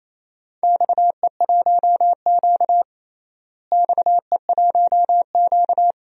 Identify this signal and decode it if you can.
Morse code, heard as a keyed tone, about 28 wpm, 700 Hz. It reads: XE1Q XE1Q
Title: DXCC-100 Call Signs - 4 Characters - Repeated 2x - 28wpm